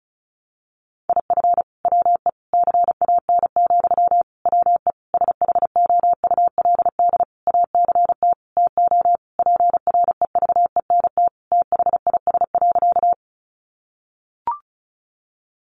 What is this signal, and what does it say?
Morse code, heard as a keyed tone, about 34 wpm, 700 Hz. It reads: IF WE CAN, WE SHOULD ACT TO PREVENT THIS.
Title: Sentences from Top 1100 Words 35wpm